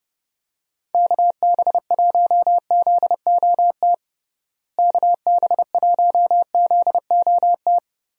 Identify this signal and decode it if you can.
Morse code, heard as a keyed tone, about 30 wpm, 700 Hz. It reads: KB1ZOT KB1ZOT
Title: USA Call Signs - 6 Characters - Repeated 3x - 30wpm